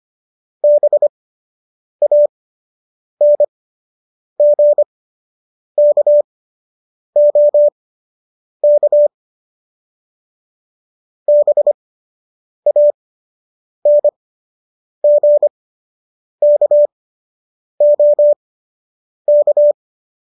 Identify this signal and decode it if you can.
Morse code, heard as a keyed tone, about 25 wpm, spaced as for 8 wpm, 590 Hz. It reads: BANGKOK BANGKOK